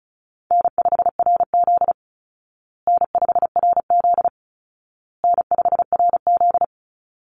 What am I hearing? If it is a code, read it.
Morse code, heard as a keyed tone, about 34 wpm, 700 Hz. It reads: N5RZ N5RZ N5RZ
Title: Call Signs Repeated 2x - 35wpm